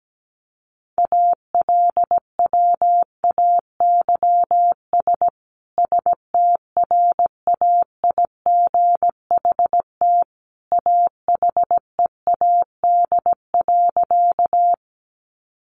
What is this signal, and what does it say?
Morse code, heard as a keyed tone, about 17 wpm, 700 Hz. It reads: ALWAYS STRAIGHT AHEAD.
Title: Sentences from Top 1100 Words 17wpm